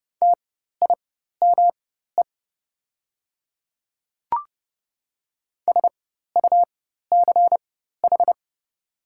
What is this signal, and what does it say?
Morse code, heard as a keyed tone, about 29 wpm, spaced as for 14 wpm, 690 Hz. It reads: TIME SUCH
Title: 4x Character Spacing - Top 200 Words - 30wpm